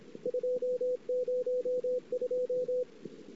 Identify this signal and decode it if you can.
Morse code, heard as a keyed tone, about 26 wpm, 500 Hz. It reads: 202
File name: ss-cw